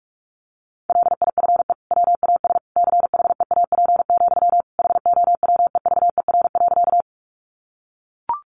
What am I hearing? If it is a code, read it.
Morse code, heard as a keyed tone, about 44 wpm, 700 Hz. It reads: LIFE WAS CHEAP, HOWEVER.
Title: Sentences from Top 1200 Words 45wpm